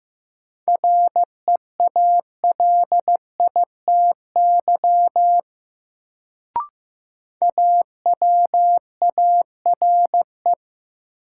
Morse code, heard as a keyed tone, about 16 wpm, 700 Hz. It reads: REALITY AWARE
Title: Sentences from Top 1200 Words - New Word Drill Alt - 15wpm